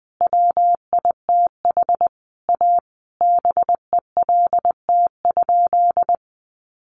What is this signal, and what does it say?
Morse code, heard as a keyed tone, about 19 wpm, 700 Hz. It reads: WITH A BELT?
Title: Sentences from Top 1400 Words 20wpm